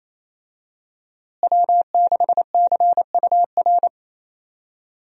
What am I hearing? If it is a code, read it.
Morse code, heard as a keyed tone, about 28 wpm, 700 Hz. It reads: W6CUR
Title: USA Call Signs - 5 Characters - Repeated 2x - 28wpm